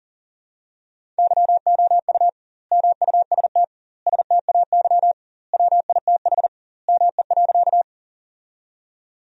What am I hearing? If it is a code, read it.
Morse code, heard as a keyed tone, about 37 wpm, 700 Hz. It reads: YOU MUST STAY WITH ME.
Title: 2x Word-Spacing - Sentences from Top 500 Words - 40wpm